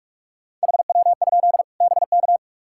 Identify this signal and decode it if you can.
Morse code, heard as a keyed tone, about 45 wpm, 700 Hz. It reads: HW? BK
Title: Practice QSO - Exchange 1 - Beginning - 45wpm